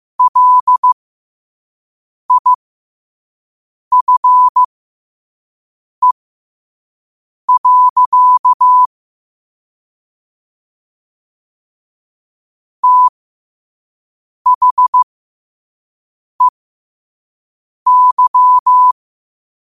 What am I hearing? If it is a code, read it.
Morse code, heard as a keyed tone, about 15 wpm, spaced as for 5 wpm, 1000 Hz. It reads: LIFE. THEY
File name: qotdfiles / 5wpm